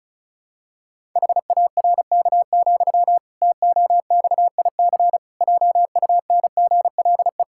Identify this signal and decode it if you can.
Morse code, heard as a keyed tone, about 36 wpm, 700 Hz. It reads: HARK, TOXIC JUNGLE
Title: Pangram Challenge v2 - 35wpm